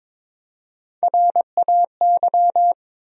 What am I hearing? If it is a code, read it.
Morse code, heard as a keyed tone, about 22 wpm, 700 Hz. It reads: RAY
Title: USA Common Names - Max 4 Letters - 22wpm